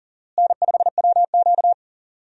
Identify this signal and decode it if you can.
Morse code, heard as a keyed tone, about 40 wpm, 700 Hz. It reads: N5WQ
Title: USA Call Signs - 4 Characters - Repeated 2x - 40wpm